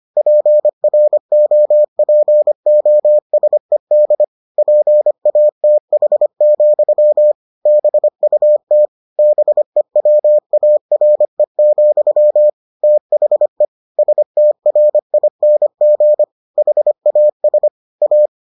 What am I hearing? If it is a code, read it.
Morse code, heard as a keyed tone, about 25 wpm, 600 Hz. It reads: PROPOSED PATH, BUT BEWARE, THE STRING HAS A